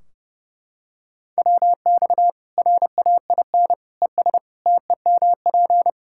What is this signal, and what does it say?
Morse code, heard as a keyed tone, about 32 wpm, 700 Hz. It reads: WX RAIN ES TEMP
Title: K1USN SST QSO Practice 30wpm